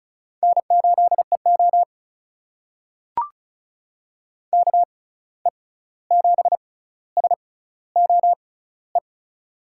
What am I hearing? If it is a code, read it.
Morse code, heard as a keyed tone, about 34 wpm, spaced as for 16 wpm, 700 Hz. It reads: N8EO KE7SOE
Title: USA Call Signs - 6x Character Spacing - 35wpm